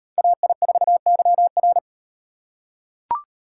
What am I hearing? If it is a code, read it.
Morse code, heard as a keyed tone, about 37 wpm, 700 Hz. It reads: AI4YR